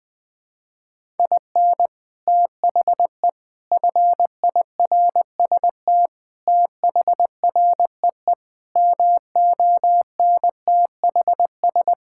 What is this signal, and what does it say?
Morse code, heard as a keyed tone, about 19 wpm, 700 Hz. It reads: IN THE FIRST THREE MONTHS
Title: Sets of 8 Words - 20wpm